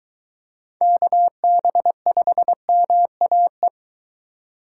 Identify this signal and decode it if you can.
Morse code, heard as a keyed tone, about 22 wpm, 700 Hz. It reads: KB5MAE